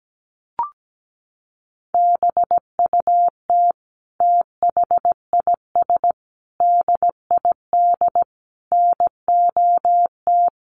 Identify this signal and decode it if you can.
Morse code, heard as a keyed tone, about 17 wpm, 700 Hz. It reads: BUT THIS DID NOT
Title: Sentences from Top 1200 Words 17wpm